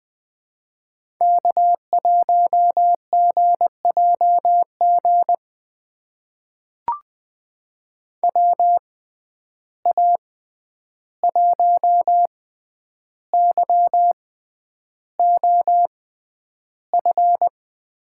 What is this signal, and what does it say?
Morse code, heard as a keyed tone, about 20 wpm, spaced as for 10 wpm, 700 Hz. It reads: K1GJG WA1YOF
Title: USA Call Signs - 6x Character Spacing - 20wpm